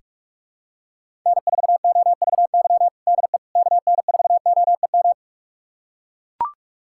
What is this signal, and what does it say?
Morse code, heard as a keyed tone, about 41 wpm, 700 Hz. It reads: N4OVY DE KN4OEM
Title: USA Callsign DE Callsign 45wpm